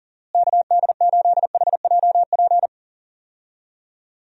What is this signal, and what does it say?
Morse code, heard as a keyed tone, about 39 wpm, 700 Hz. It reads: KD8HJP
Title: USA Call Signs - 6 Characters - Repeated 2x - 40wpm